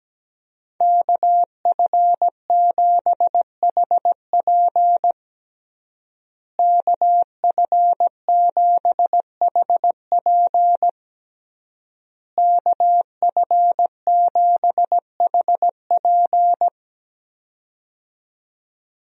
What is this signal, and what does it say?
Morse code, heard as a keyed tone, about 17 wpm, 700 Hz. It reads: KF7HP KF7HP KF7HP
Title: USA Call Signs - 5 Characters - Repeated 3x - 17wpm